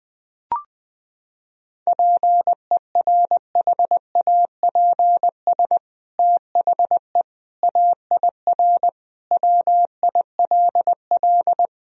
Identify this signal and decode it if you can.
Morse code, heard as a keyed tone, about 21 wpm, 700 Hz. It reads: PERHAPS THE AIR WILL
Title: Sentences from Top 1000 Words - Review Part 1 - 20wpm